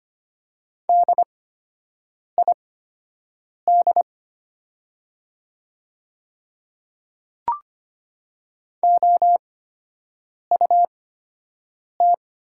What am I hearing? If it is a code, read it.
Morse code, heard as a keyed tone, about 25 wpm, spaced as for 7 wpm, 700 Hz. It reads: DID OUT
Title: 8x Character Spacing - Top 200 Words - 25wpm